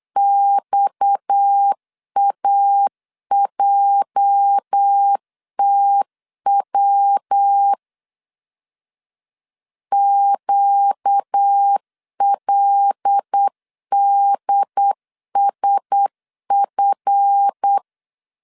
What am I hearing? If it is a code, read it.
Morse code, heard as a keyed tone, about 8 wpm, 800 Hz. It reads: XAJTW QLDSF
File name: LEZIONI / 8 YJ45 / 8 YJ 8wpm 3313